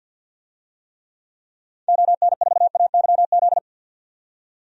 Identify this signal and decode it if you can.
Morse code, heard as a keyed tone, about 47 wpm, 700 Hz. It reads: ON4AYZ
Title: DXCC-100 Call Signs - 6 Characters - 50wpm